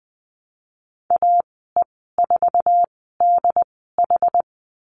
Morse code, heard as a keyed tone, about 20 wpm, spaced as for 15 wpm, 700 Hz. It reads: AE4DH
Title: 076 Lesson 13g - Letter L - Call Signs 20wpm